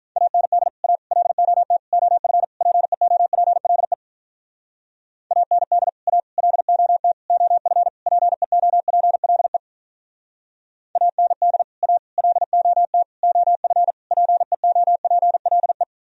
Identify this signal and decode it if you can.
Morse code, heard as a keyed tone, about 46 wpm, 700 Hz. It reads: AND A LOT OF PEOPLE AND A LOT OF PEOPLE AND A LOT OF PEOPLE